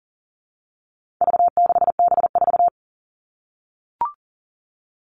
Morse code, heard as a keyed tone, about 40 wpm, 700 Hz. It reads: V6B4
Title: All Cut Numbers - 4-Digit 40wpm